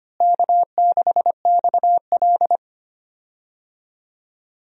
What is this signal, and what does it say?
Morse code, heard as a keyed tone, about 25 wpm, 700 Hz. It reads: K6XL
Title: USA Call Signs - 4 Characters - Repeated 2x - 25wpm